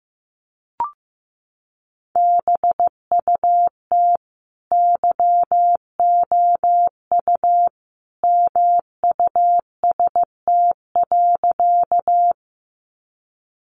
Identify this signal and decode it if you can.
Morse code, heard as a keyed tone, about 15 wpm, 700 Hz. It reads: BUT YOU MUST.
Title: Sentences from Top 200 Words 15wpm